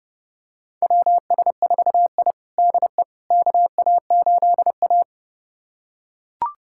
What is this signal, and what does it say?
Morse code, heard as a keyed tone, about 32 wpm, 700 Hz. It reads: WS4I DE KA8A
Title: USA Callsign DE Callsign - 4 Characters - 30wpm